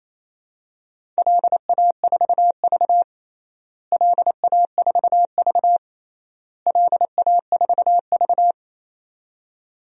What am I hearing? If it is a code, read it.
Morse code, heard as a keyed tone, about 28 wpm, 700 Hz. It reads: LA4V LA4V LA4V
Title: DXCC-100 Call Signs - 4 Characters - Repeated 3x - 28wpm